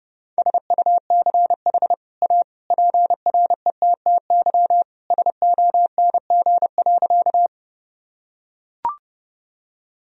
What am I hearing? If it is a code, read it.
Morse code, heard as a keyed tone, about 32 wpm, 700 Hz. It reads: SUCH A PRETTY SONG.
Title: Sentences from Top 1000 Words - Review Part 2 - 30wpm